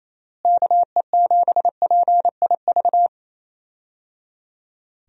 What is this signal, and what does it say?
Morse code, heard as a keyed tone, about 28 wpm, 700 Hz. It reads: KE7PIV
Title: USA Call Signs - 6 Characters - Repeated 3x - 28wpm